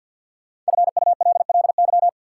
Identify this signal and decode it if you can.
Morse code, heard as a keyed tone, about 53 wpm, 700 Hz. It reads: FULLY
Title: Sentences from Top 1300 Words - New Word Drill - 50wpm